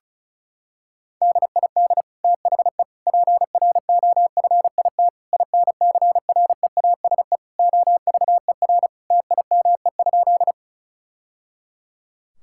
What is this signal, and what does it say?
Morse code, heard as a keyed tone, about 34 wpm, 700 Hz. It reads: DID THE PROFIT INCREASE OVER TIME?
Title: Sentences from Top 1000 Words - Review Part 5 - 35wpm